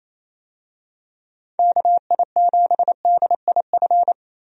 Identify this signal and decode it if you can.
Morse code, heard as a keyed tone, about 28 wpm, 700 Hz. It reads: KI7DIF
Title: USA Call Signs - 6 Characters - Repeated 3x - 28wpm